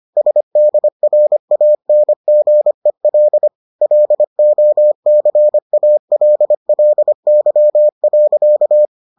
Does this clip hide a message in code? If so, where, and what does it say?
Yes: Morse code from 0.2 to 8.9 s, heard as a keyed tone, about 25 wpm, 600 Hz. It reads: SDRANGEL LOCALLY.